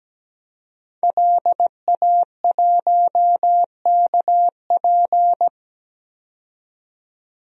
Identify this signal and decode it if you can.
Morse code, heard as a keyed tone, about 17 wpm, 700 Hz. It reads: LA1KP
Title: DXCC-100 Call Signs - 5 Characters - 17wpm